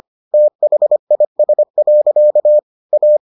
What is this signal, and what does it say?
Morse code, heard as a keyed tone, about 25 wpm, 600 Hz. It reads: THIS. A